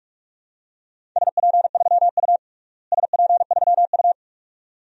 Morse code, heard as a keyed tone, about 46 wpm, 700 Hz. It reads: SP3U SP3U
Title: DXCC-100 Call Signs - 4 Characters - Repeated 2x - 45wpm